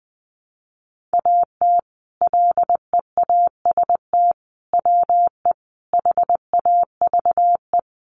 Morse code, heard as a keyed tone, about 19 wpm, 700 Hz. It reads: AT LEAST WE HAVE
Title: Sentences from Top 1300 Words 20wpm